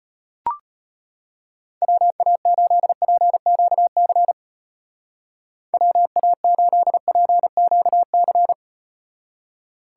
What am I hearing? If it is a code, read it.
Morse code, heard as a keyed tone, about 38 wpm, 700 Hz. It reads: WA8PQC WA8PQC